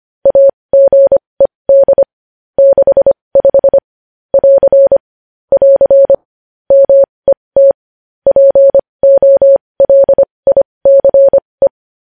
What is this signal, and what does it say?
Morse code, heard as a keyed tone, about 25 wpm, spaced as for 20 wpm, 550 Hz. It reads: AGED 65 <AR> <AR> MET POLICE
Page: News Headlines in Morse Code at 20 WPM